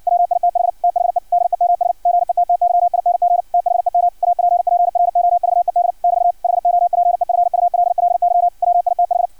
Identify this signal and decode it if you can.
Morse code, heard as a keyed tone, about 56 wpm, 700 Hz. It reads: WITH THE DEMS GETTYING THEM APPROVED <BT> HOPEFULLY WITH
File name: archived / CRSnet